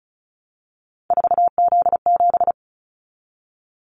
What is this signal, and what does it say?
Morse code, heard as a keyed tone, about 36 wpm, 700 Hz. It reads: 4Z7
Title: Callsign Prefixes for Top 50 DXCC - 35wpm